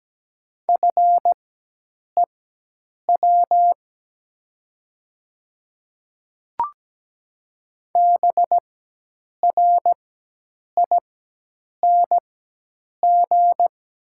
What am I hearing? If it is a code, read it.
Morse code, heard as a keyed tone, about 17 wpm, spaced as for 8 wpm, 700 Hz. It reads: FEW BRING
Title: 4x Character Spacing - Top 500 Words - 17wpm